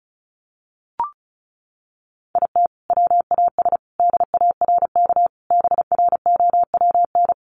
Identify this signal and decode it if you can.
Morse code, heard as a keyed tone, about 34 wpm, 700 Hz. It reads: IT WAS DARK BROWN
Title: Sentences from Top 1400 Words 35wpm